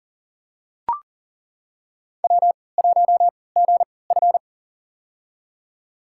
Morse code, heard as a keyed tone, about 40 wpm, spaced as for 23 wpm, 700 Hz. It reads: W1GF
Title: USA Call Signs - 3x Character Spacing - 40wpm